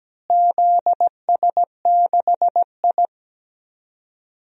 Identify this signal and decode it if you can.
Morse code, heard as a keyed tone, about 17 wpm, 700 Hz. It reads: ZS6I
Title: DXCC-100 Call Signs - 4 Characters - Repeated 2x - 17wpm